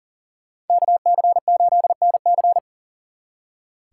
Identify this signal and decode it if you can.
Morse code, heard as a keyed tone, about 52 wpm, 700 Hz. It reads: KC8NC